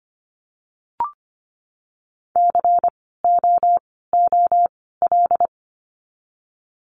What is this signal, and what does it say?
Morse code, heard as a keyed tone, about 25 wpm, spaced as for 16 wpm, 700 Hz. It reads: COOL
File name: course / 25:10wpm / 094 Lesson 16f - Letter c - Words 25wpm